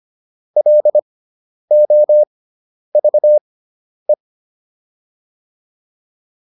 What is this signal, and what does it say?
Morse code, heard as a keyed tone, about 25 wpm, spaced as for 10 wpm, 600 Hz. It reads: LOVE